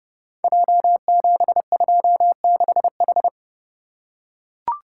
Morse code, heard as a keyed tone, about 30 wpm, 700 Hz. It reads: J726H
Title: Code Groups - 5 Letter:Numbers - 30wpm